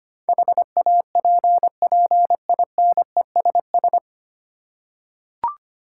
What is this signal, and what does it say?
Morse code, heard as a keyed tone, about 25 wpm, 700 Hz. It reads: HAPPINESS
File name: practice / 25wpm / 9 Letter Words - First 3 Letters Given - 25wpm